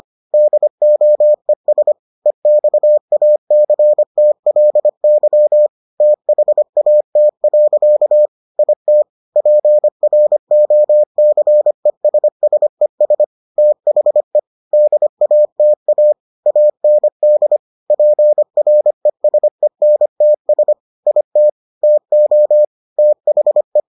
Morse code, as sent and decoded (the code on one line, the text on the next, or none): -.. --- . ... / . -..- .- -.-. - .-.. -.-- / - .... .- - .-.-.- / .. - / .--. .-. --- -.-. ... ... ... / - .... / -.. .- - .- / .- -. -.. / .--. .-. ... -. - ... / .. - / - --- / - .... .
DOES EXACTLY THAT. IT PROCESSES THE DATA AND PRESENTS IT TO THE